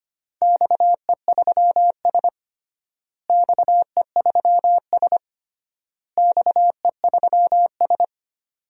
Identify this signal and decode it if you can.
Morse code, heard as a keyed tone, about 25 wpm, 700 Hz. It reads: XE3S XE3S XE3S
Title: DXCC-100 Call Signs - 4 Characters - Repeated 3x - 25wpm